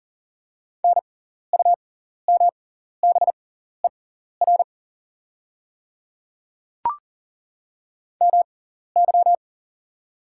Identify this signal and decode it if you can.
Morse code, heard as a keyed tone, about 38 wpm, spaced as for 14 wpm, 690 Hz. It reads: NUMBER MY
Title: 6x Character Spacing - Top 300 Words - 40wpm